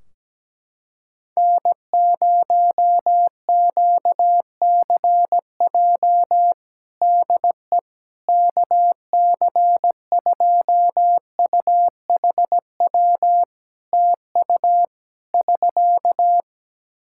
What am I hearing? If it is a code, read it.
Morse code, heard as a keyed tone, about 17 wpm, 700 Hz. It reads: N0QCJ DE KC2UHW TU <SK>
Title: K1USN SST QSO Practice 17wpm